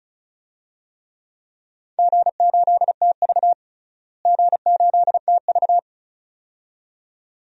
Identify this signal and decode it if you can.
Morse code, heard as a keyed tone, about 34 wpm, 700 Hz. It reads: G8TV G8TV
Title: DXCC-100 Call Signs - 4 Characters - Repeated 2x - 35wpm